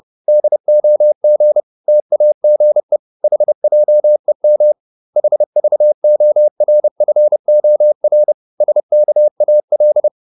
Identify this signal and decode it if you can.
Morse code, heard as a keyed tone, about 30 wpm, 600 Hz. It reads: DOG TAGE HJEM HVORFOR SKAL
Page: Ferie 30 wpm